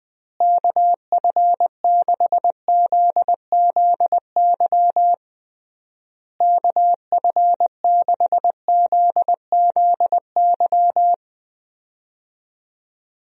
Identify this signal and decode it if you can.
Morse code, heard as a keyed tone, about 21 wpm, 700 Hz. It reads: KF6ZZY KF6ZZY
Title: USA Call Signs - 6 Characters - Repeated 3x - 20wpm